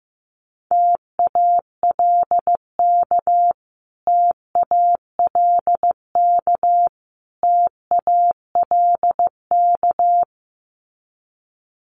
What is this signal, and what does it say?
Morse code, heard as a keyed tone, about 15 wpm, 700 Hz. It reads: TALK TALK TALK
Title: Top 500 Words Repeated 15wpm